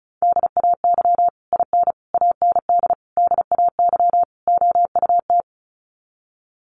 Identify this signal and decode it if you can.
Morse code, heard as a keyed tone, about 34 wpm, 700 Hz. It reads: DAY IN AND DAY OUT
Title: Sets of 5 Words Encore2 35wpm